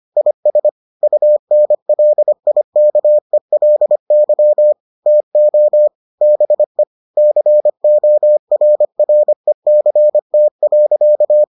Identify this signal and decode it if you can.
Morse code, heard as a keyed tone, about 25 wpm, 600 Hz. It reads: IS UNLIKELY TO BE CORRECT.